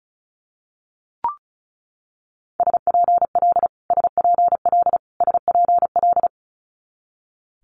Morse code, heard as a keyed tone, about 36 wpm, 700 Hz. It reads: SPL SPL SPL
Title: Multi-letter Phonemes Repeated 3x 35wpm